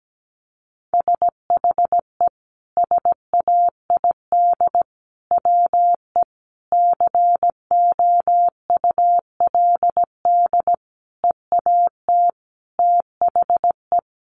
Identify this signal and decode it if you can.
Morse code, heard as a keyed tone, about 17 wpm, 700 Hz. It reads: SHE SAID WE COULD EAT THE
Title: Sentences from Top 1300 Words 17wpm